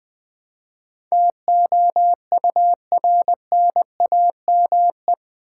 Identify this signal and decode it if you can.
Morse code, heard as a keyed tone, about 20 wpm, 700 Hz. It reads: TOURNAME
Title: Build A Word - 4 Letters At A Time - 10-Letter Words - 20wpm